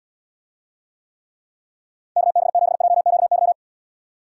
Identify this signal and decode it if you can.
Morse code, heard as a keyed tone, about 76 wpm, 700 Hz. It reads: KB7JQY